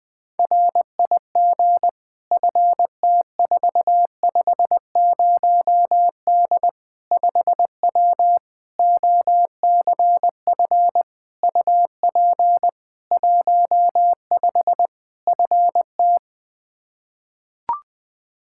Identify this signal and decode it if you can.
Morse code, heard as a keyed tone, about 21 wpm, 700 Hz. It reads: RIG FT450D 5W OCF UP 15 FT
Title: K1USN SST QSO Practice 20wpm